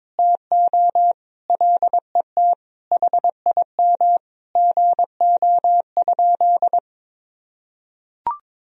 Morse code, heard as a keyed tone, about 22 wpm, 700 Hz. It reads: TO LET HIM GO?
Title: Sentences from Top 1000 Words - Review Part 4 - 22wpm